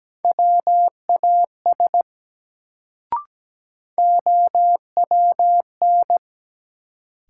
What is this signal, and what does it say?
Morse code, heard as a keyed tone, about 17 wpm, 700 Hz. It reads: WAS OWN
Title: Sets of 2 Words - New Word Drill - 17wpm